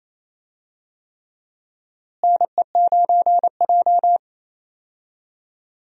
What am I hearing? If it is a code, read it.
Morse code, heard as a keyed tone, about 28 wpm, 700 Hz. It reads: NE9J
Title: USA Call Signs - 4 Characters - Repeated 3x - 28wpm